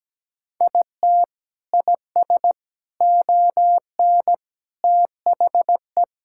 Morse code, heard as a keyed tone, about 17 wpm, 700 Hz. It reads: IT IS ON THE
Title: Sentences from Top 1000 Words - Review Part 6 - 17wpm